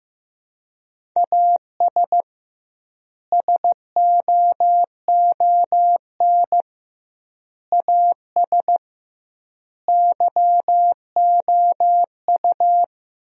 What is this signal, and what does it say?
Morse code, heard as a keyed tone, about 15 wpm, 700 Hz. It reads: AS SOON AS YOU
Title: 2x Word-Spacing - Sentences from Top 400 Words - 15wpm